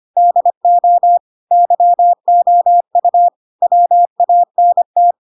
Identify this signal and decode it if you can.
Morse code, heard as a keyed tone, about 25 wpm, 700 Hz. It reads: DO YOU WANT